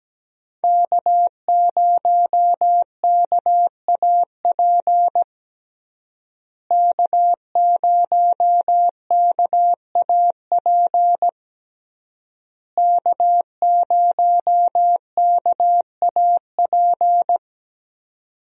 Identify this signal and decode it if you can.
Morse code, heard as a keyed tone, about 17 wpm, 700 Hz. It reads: K0KAP K0KAP K0KAP
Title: USA Call Signs - 5 Characters - Repeated 3x - 17wpm